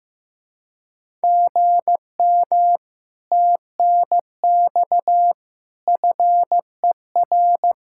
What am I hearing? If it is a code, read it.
Morse code, heard as a keyed tone, about 15 wpm, 700 Hz. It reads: GM TNX FER
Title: Speed-Racing Common QSO Phrases Encore - 15wpm